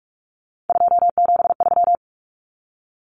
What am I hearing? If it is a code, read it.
Morse code, heard as a keyed tone, about 44 wpm, 700 Hz. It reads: 273
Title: Common Cut Numbers - 3-Digit 45wpm